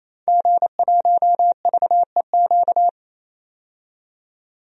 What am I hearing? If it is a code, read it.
Morse code, heard as a keyed tone, about 27 wpm, 700 Hz. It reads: G1VEQ
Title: DXCC-100 Call Signs - 5 Characters - 28wpm